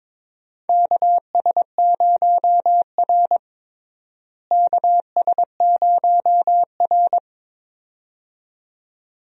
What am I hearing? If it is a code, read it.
Morse code, heard as a keyed tone, about 22 wpm, 700 Hz. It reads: KS0R KS0R
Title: USA Call Signs - 4 Characters - Repeated 2x - 22wpm